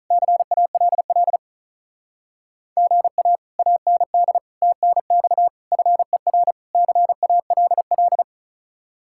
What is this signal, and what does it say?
Morse code, heard as a keyed tone, about 45 wpm, 700 Hz. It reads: CALL GA AND TNX FER CALL